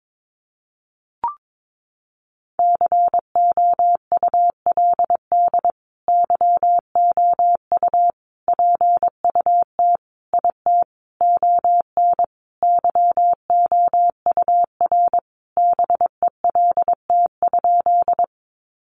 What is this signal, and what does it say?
Morse code, heard as a keyed tone, about 22 wpm, 700 Hz. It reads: COULD YOU PUT IT ON YOUR BELT?
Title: Sentences from Top 1400 Words 22wpm